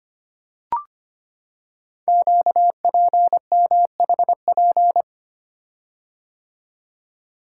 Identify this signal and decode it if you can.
Morse code, heard as a keyed tone, about 25 wpm, 700 Hz. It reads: QPMHP
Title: Code Groups - 5 Letters - 25wpm